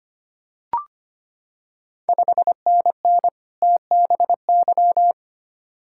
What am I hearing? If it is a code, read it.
Morse code, heard as a keyed tone, about 25 wpm, 690 Hz. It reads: 5NN TBY
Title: Ontario QSO Party Exchange 25wpm